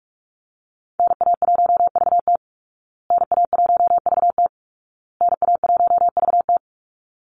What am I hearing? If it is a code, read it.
Morse code, heard as a keyed tone, about 44 wpm, 700 Hz. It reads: NA1VT NA1VT NA1VT
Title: Call Signs Repeated 2x - 45wpm